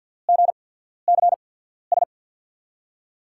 Morse code, heard as a keyed tone, about 52 wpm, spaced as for 14 wpm, 700 Hz. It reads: GCS
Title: USA Call Signs - 5 Characters - 8x Character Spacing - 50wpm